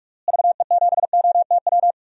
Morse code, heard as a keyed tone, about 44 wpm, 700 Hz. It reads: VE7OTW
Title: DXCC-100 Call Signs - 6 Characters - Repeated 3x - 45wpm